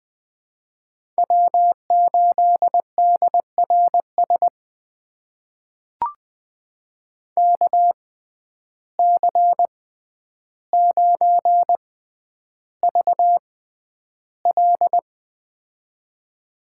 Morse code, heard as a keyed tone, about 20 wpm, spaced as for 10 wpm, 700 Hz. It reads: W8DRS KC9VL
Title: USA Call Signs - 5 Characters - 6x Character Spacing - 20wpm